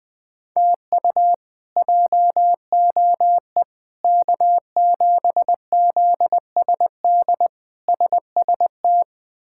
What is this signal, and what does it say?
Morse code, heard as a keyed tone, about 20 wpm, 700 Hz. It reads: TU JOE K7ZSD SST
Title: K1USN SST QSO Practice - 20wpm